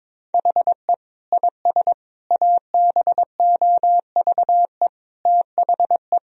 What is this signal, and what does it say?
Morse code, heard as a keyed tone, about 22 wpm, 700 Hz. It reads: HE IS ABOVE THE
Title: Sentences from Top 1000 Words - Review Part 3 - 22wpm